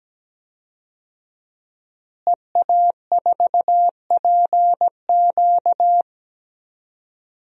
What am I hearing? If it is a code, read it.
Morse code, heard as a keyed tone, about 17 wpm, 700 Hz. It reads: EA4PQ
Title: DXCC-100 Call Signs - 5 Characters - Repeated 2x - 17wpm